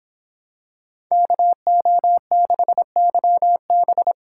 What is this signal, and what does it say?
Morse code, heard as a keyed tone, about 25 wpm, 700 Hz. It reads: KO6YB